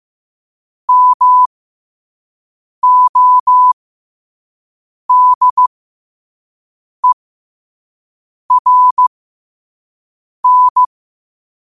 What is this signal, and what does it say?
Morse code, heard as a keyed tone, about 15 wpm, spaced as for 5 wpm, 1000 Hz. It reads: MODERN
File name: qotdfiles / 5wpm